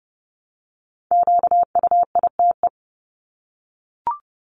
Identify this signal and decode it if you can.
Morse code, heard as a keyed tone, about 30 wpm, 700 Hz. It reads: QUITE
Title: Top 501-750 Words 30wpm